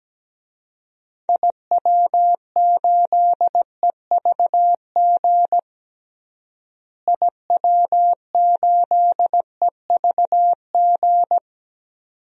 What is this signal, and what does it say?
Morse code, heard as a keyed tone, about 17 wpm, 700 Hz. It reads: IW8EVG IW8EVG
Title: DXCC-100 Call Signs - 6 Characters - Repeated 2x - 17wpm